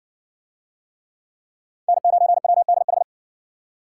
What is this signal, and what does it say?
Morse code, heard as a keyed tone, about 52 wpm, 700 Hz. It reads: N9WDH